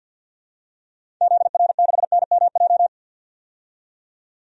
Morse code, heard as a keyed tone, about 51 wpm, 700 Hz. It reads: ZR6NMJ
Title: DXCC-100 Call Signs - 6 Characters - 50wpm